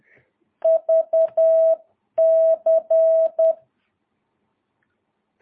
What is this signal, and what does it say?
Morse code, heard as a keyed tone, about 10 wpm, 650 Hz. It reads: VC
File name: jh8cbh / 2022cw